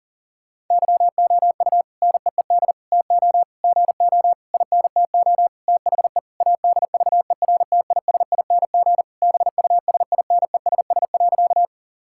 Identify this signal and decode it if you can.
Morse code, heard as a keyed tone, about 40 wpm, 700 Hz. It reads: YOU NEED TO GO INTO THE ADVERTISING BUSINESS.
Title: Sentences from Top 1000 Words - Review Part 6 - 40wpm